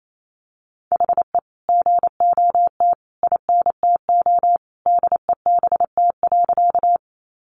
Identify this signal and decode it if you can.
Morse code, heard as a keyed tone, about 28 wpm, 700 Hz. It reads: HE GOT INTO DEBT.
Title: Sentences from Top 1300 Words 28wpm